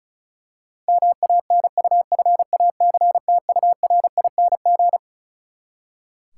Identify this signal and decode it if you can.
Morse code, heard as a keyed tone, about 36 wpm, 700 Hz. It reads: MANUFACTURING
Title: K1USN SST QSO Practice 35wpm